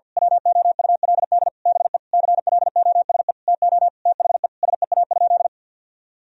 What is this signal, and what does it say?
Morse code, heard as a keyed tone, about 53 wpm, 700 Hz. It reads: WOULD BE CLOSE TO THE SEA?
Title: Sentences from Top 1000 Words - Review Part 1 - 50wpm